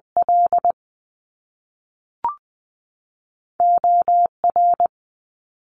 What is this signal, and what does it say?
Morse code, heard as a keyed tone, about 20 wpm, spaced as for 5 wpm, 700 Hz. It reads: LOR
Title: US States 20wpm